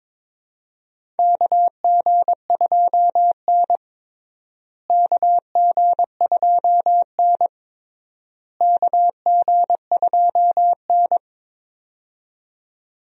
Morse code, heard as a keyed tone, about 22 wpm, 700 Hz. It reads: KG2N KG2N KG2N
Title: USA Call Signs - 4 Characters - Repeated 3x - 22wpm